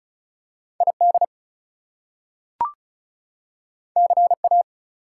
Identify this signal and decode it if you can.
Morse code, heard as a keyed tone, about 36 wpm, 700 Hz. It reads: ID CA
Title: US States and Canadian Provinces 35wpm